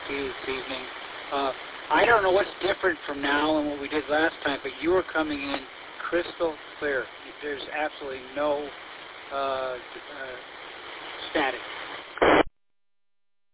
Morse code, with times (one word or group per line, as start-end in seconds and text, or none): none